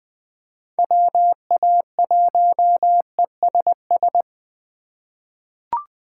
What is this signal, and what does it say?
Morse code, heard as a keyed tone, about 20 wpm, 700 Hz. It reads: WA1ESS
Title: USA Call Signs - 2x Character Spacing - 20wpm